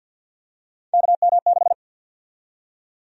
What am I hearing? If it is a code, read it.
Morse code, heard as a keyed tone, about 53 wpm, 700 Hz. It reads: KM6
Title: USA Call Sign Prefixes - Repeated 2x - 50wpm